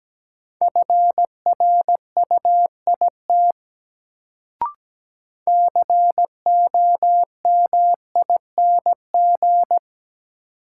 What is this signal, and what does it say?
Morse code, heard as a keyed tone, about 17 wpm, 700 Hz. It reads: FRUIT COMING
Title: Sentences from Top 1300 Words - New Word Drill - 17wpm